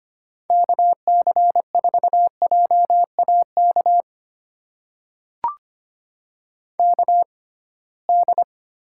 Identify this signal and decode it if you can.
Morse code, heard as a keyed tone, about 25 wpm, spaced as for 19 wpm, 700 Hz. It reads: KC4JAK KD
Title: USA Call Signs - 6 Characters - 6x Character Spacing - 25wpm